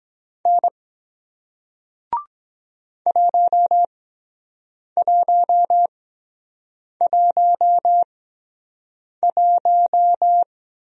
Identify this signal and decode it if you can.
Morse code, heard as a keyed tone, about 23 wpm, spaced as for 7 wpm, 700 Hz. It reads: N 1111